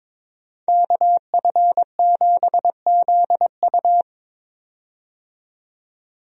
Morse code, heard as a keyed tone, about 22 wpm, 700 Hz. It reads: KF7ZU
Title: USA Call Signs - 5 Characters - 22wpm